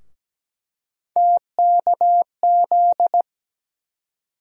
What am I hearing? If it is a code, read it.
Morse code, heard as a keyed tone, about 17 wpm, 700 Hz. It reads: TKZ
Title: USA Call Sign Postfixes - 3 Letters 17wpm